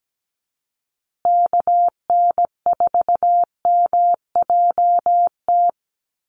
Morse code, heard as a keyed tone, about 17 wpm, 700 Hz. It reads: KN4MJT
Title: USA Call Signs Repeated 3x 17wpm